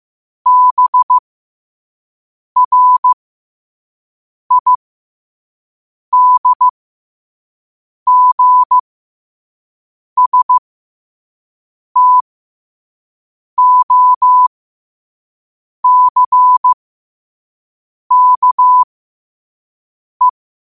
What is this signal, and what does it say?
Morse code, heard as a keyed tone, about 15 wpm, spaced as for 5 wpm, 1000 Hz. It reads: BRIDGSTOCKE
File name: qotdfiles / 5wpm